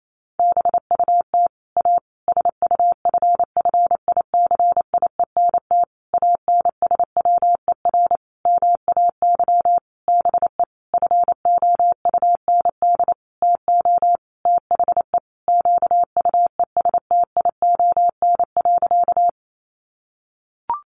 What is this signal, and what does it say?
Morse code, heard as a keyed tone, about 28 wpm, 700 Hz. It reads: BUT A SUFFICIENT ANSWER MAY BE FOUND TO THE QUESTION.
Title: Sentences from Top 1400 Words 28wpm